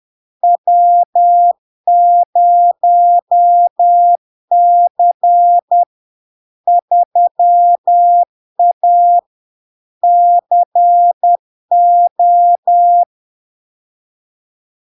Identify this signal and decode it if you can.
Morse code, heard as a keyed tone, about 10 wpm, 700 Hz. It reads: W0C 3A CO
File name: LARK-FD-2022-Log-in-CW-10wpm-700-Hz